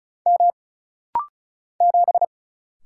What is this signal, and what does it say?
Morse code, heard as a keyed tone, about 36 wpm, 700 Hz. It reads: M 7
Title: Single Letter-Number - Rapid-Fire 35wpm